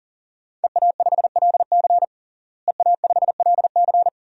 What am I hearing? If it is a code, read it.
Morse code, heard as a keyed tone, about 37 wpm, 700 Hz. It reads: EA5LC EA5LC
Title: DXCC-100 Call Signs - 5 Characters - Repeated 2x - 40wpm